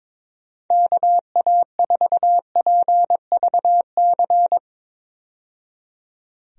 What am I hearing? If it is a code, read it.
Morse code, heard as a keyed tone, about 22 wpm, 700 Hz. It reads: KA4PVC
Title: USA Call Signs - 6 Characters - 22wpm